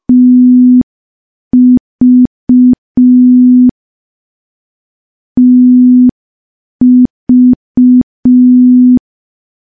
Morse code, heard as a keyed tone, about 5 wpm, 250 Hz. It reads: TV TV
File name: morse / morse-training-05WPM